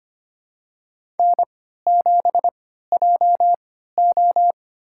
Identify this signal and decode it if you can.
Morse code, heard as a keyed tone, about 25 wpm, spaced as for 14 wpm, 700 Hz. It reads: N7JO
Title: USA Call Signs - 4 Characters - 3x Character Spacing - 25wpm